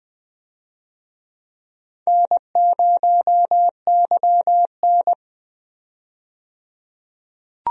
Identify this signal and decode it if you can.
Morse code, heard as a keyed tone, about 20 wpm, 700 Hz. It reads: N0YN
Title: USA Call Signs - 4 Characters - 20wpm